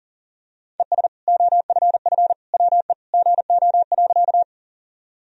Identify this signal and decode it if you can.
Morse code, heard as a keyed tone, about 40 wpm, 700 Hz. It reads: ES OFF WE GO.
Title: Sentences from Top 200 Words and Abbreviations 40wpm